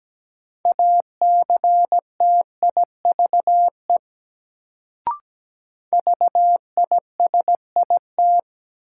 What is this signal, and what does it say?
Morse code, heard as a keyed tone, about 17 wpm, 700 Hz. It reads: ACTIVE VISIT
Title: Sentences from Top 900 Words - New Word Drill - 17wpm